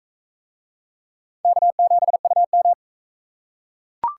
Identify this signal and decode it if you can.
Morse code, heard as a keyed tone, about 40 wpm, 700 Hz. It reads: K7UM